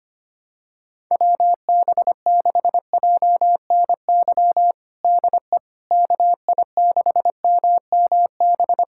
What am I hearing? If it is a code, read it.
Morse code, heard as a keyed tone, about 25 wpm, 700 Hz. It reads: WB6JNY DE KI6MMB
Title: USA Callsign DE Callsign 25wpm